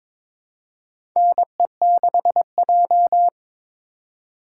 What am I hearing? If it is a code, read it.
Morse code, heard as a keyed tone, about 22 wpm, 700 Hz. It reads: NE6J
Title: USA Call Signs - 4x Character Spacing - 22wpm